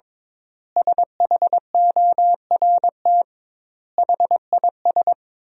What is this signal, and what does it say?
Morse code, heard as a keyed tone, about 22 wpm, 700 Hz. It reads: SHORT HIS
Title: 2x Word-Spacing - Sentences from Top 400 Words - 22wpm